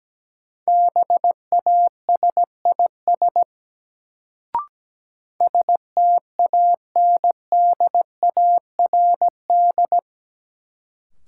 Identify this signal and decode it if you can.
Morse code, heard as a keyed tone, about 17 wpm, 700 Hz. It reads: BASIS STANDARD
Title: Sentences from Top 1200 Words - New Word Drill Alt - 17wpm